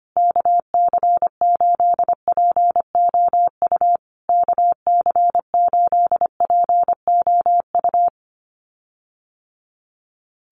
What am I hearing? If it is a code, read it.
Morse code, heard as a keyed tone, about 25 wpm, 700 Hz. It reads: KC8POU KC8POU
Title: USA Call Signs Repeated 3x 25wpm